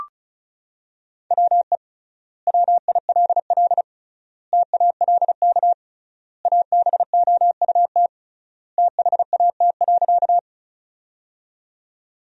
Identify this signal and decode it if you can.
Morse code, heard as a keyed tone, about 36 wpm, 700 Hz. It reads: WE WILL TALK ABOUT THAT.
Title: 3x Word-Spacing - Sentences from Top 400 Words - 35wpm